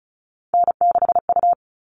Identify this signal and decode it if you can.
Morse code, heard as a keyed tone, about 34 wpm, 700 Hz. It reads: N6U
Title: All Cut Numbers - 3-Digit 35wpm